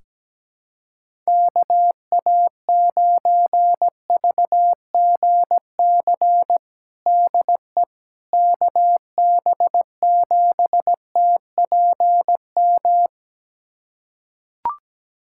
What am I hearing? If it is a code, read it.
Morse code, heard as a keyed tone, about 17 wpm, 700 Hz. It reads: KA9VGC DE KB7TPM
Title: USA Callsign DE Callsign - 6 Characters - 17wpm